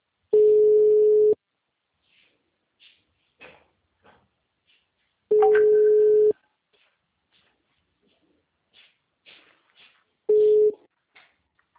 Ringback tone (Europe, 425 Hz).
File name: out- -2029- - - .6